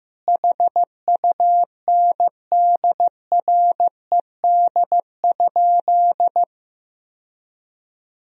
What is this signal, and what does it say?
Morse code, heard as a keyed tone, about 15 wpm, 700 Hz. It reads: HUNDRED?
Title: Sentences from Top 1000 Words - Review Part 2 - 15wpm